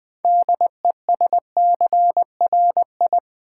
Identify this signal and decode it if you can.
Morse code, heard as a keyed tone, about 20 wpm, 700 Hz. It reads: DESCRI
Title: Build A Word - 3 Letters At A Time - 8-Letter Words - 20wpm